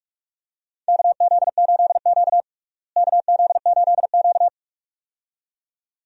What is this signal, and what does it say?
Morse code, heard as a keyed tone, about 46 wpm, 700 Hz. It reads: KZ8Q KZ8Q
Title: USA Call Signs - 4 Characters - Repeated 2x - 45wpm